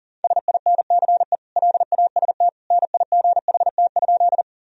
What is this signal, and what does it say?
Morse code, heard as a keyed tone, about 40 wpm, 700 Hz. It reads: SINCE LAST NIGHT?
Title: Sentences from Top 1000 Words - Review Part 1 - 40wpm